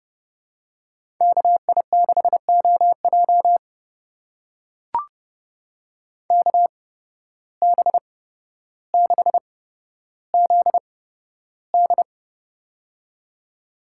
Morse code, heard as a keyed tone, about 32 wpm, spaced as for 13 wpm, 700 Hz. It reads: KI6OJ KB6ZD
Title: USA Call Signs - 5 Characters - 8x Character Spacing - 30wpm